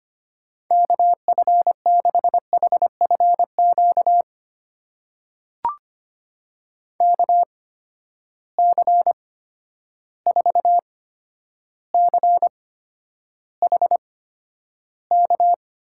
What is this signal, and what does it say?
Morse code, heard as a keyed tone, about 25 wpm, spaced as for 11 wpm, 700 Hz. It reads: KF6HFQ KC4CHK
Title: USA Call Signs - 6 Characters - 8x Character Spacing - 25wpm